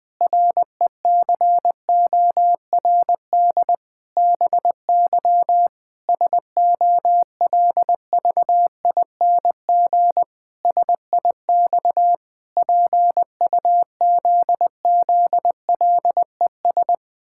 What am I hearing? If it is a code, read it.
Morse code, heard as a keyed tone, about 20 wpm, 700 Hz. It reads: RECORD BY SOLVING SIX PUZZLES
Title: Pangram Challenge v2 - 20wpm